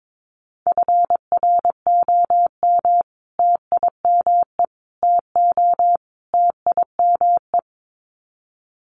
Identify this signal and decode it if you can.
Morse code, heard as a keyed tone, about 22 wpm, 700 Hz. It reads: FROM TIME TO TIME
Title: Sets of 4 Words 22wpm